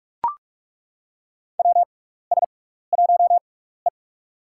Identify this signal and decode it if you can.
Morse code, heard as a keyed tone, about 42 wpm, spaced as for 15 wpm, 700 Hz. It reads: WS1E
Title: USA Call Signs - 4 Characters - 6x Character Spacing - 45wpm